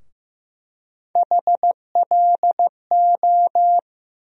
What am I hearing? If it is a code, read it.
Morse code, heard as a keyed tone, about 15 wpm, 700 Hz. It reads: HLO
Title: USA Call Sign Postfixes - 3 Letters 15wpm